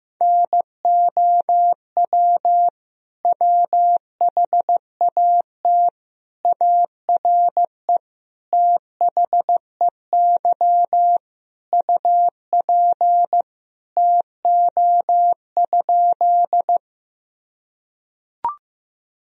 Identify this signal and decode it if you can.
Morse code, heard as a keyed tone, about 15 wpm, 700 Hz. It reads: NOW WHAT ARE THEY UP TO?
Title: Sentences from Top 1000 Words - Review Part 1 - 15wpm